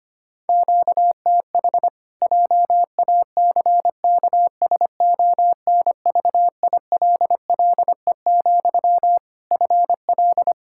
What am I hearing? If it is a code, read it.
Morse code, heard as a keyed tone, about 25 wpm, 700 Hz. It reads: QTH JACKSONVILLE, FL
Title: Practice QSO - Exchange 1 - Beginning - 25wpm